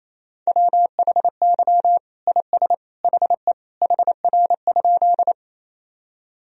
Morse code, heard as a keyed tone, about 28 wpm, 700 Hz. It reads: WHY IS HE HR?
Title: Sentences from Top 200 Words and Abbreviations 28wpm